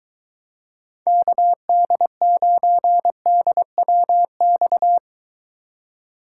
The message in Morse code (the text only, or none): KD9DWX